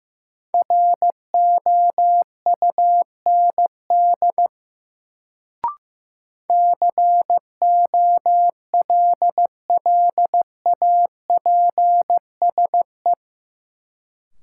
Morse code, heard as a keyed tone, about 15 wpm, 700 Hz. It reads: ROUND COLLAPSE
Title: Sets of 5 Words Encore - New Word Drill - 15wpm